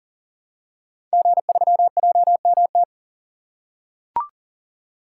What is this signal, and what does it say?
Morse code, heard as a keyed tone, about 39 wpm, 700 Hz. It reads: G3JMT
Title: Code Groups - 5 Letter:Numbers - 40wpm